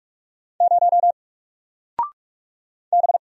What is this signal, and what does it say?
Morse code, heard as a keyed tone, about 46 wpm, 700 Hz. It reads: 0 B
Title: Single Character (Letter:Number:Punctuation:Prosign) Alt 45wpm